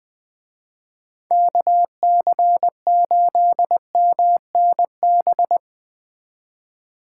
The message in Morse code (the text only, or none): KC8MNB